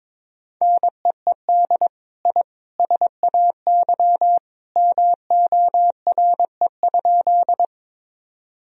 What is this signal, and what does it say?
Morse code, heard as a keyed tone, about 22 wpm, 700 Hz. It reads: NEED I SAY MORE?
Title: Sentences from Top 1000 Words - Review Part 1 - 22wpm